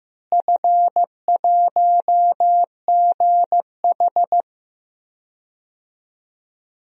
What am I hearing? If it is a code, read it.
Morse code, heard as a keyed tone, about 16 wpm, 700 Hz. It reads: F1GH
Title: DXCC-100 Call Signs - 4 Characters - Repeated 2x - 15wpm